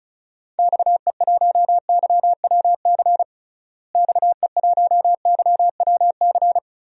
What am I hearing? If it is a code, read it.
Morse code, heard as a keyed tone, about 35 wpm, 700 Hz. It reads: XE1YWC XE1YWC
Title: DXCC-100 Call Signs - 6 Characters - Repeated 3x - 35wpm